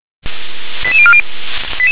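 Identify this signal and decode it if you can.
Five-tone selective call (ZVEI), heard as quick runs of five tones, tones of about 70 ms. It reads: 88040
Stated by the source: ZVEI five-tone selective call 88162 received with an SDR.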